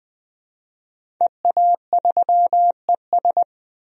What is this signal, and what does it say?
Morse code, heard as a keyed tone, about 20 wpm, 700 Hz. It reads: EA3ES
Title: DXCC-100 Call Signs - 5 Characters - Repeated 2x - 20wpm